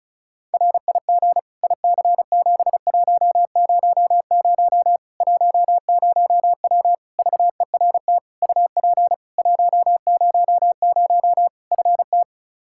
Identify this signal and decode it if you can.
Morse code, heard as a keyed tone, about 36 wpm, 700 Hz. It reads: RIG IC7100 10W VERT UP 100 FT
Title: Practice QSO - Exchange 2 - Info - 35wpm